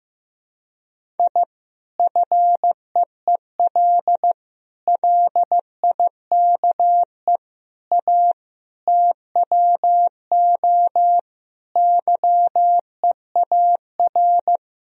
Morse code, heard as a keyed tone, about 15 wpm, 700 Hz. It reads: I FEEL LIKE A TWO YEAR
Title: Sentences from Top 1000 Words - Review Part 1 - 15wpm